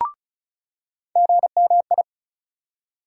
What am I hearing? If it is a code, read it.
Morse code, heard as a keyed tone, about 34 wpm, 700 Hz. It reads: GMI
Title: USA Call Sign Postfixes - 3 Letters 35wpm